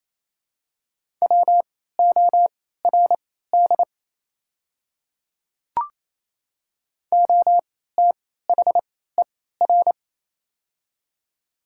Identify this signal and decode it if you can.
Morse code, heard as a keyed tone, about 28 wpm, spaced as for 16 wpm, 700 Hz. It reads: WORD OTHER
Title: 3x Character Spacing - Top 100 Words - 28wpm